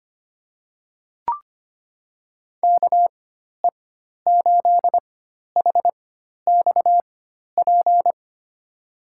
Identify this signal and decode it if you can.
Morse code, heard as a keyed tone, about 25 wpm, spaced as for 12 wpm, 700 Hz. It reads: KE8HXP
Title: USA Call Signs - 6 Characters - 4x Character Spacing - 25wpm